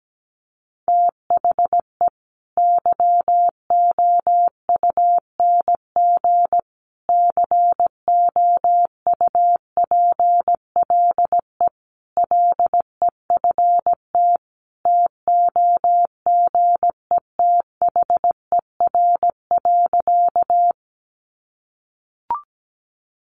Morse code, heard as a keyed tone, about 17 wpm, 700 Hz. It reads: THE YOUNG COUPLE LEFT TOGETHER.
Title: Sentences from Top 1100 Words 17wpm